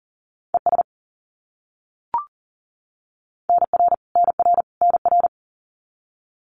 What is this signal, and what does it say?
Morse code, heard as a keyed tone, about 40 wpm, 700 Hz. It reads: ES NR NR NR
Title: Abbreviations Repeated 3x - 40wpm